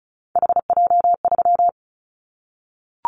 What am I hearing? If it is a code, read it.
Morse code, heard as a keyed tone, about 36 wpm, 700 Hz. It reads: HJ3
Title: Callsign Prefixes for Top 100 DXCC - 35wpm